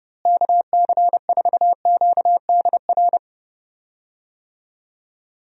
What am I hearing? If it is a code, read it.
Morse code, heard as a keyed tone, about 30 wpm, 700 Hz. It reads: KC4QDR
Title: USA Call Signs - 6 Characters - Repeated 2x - 30wpm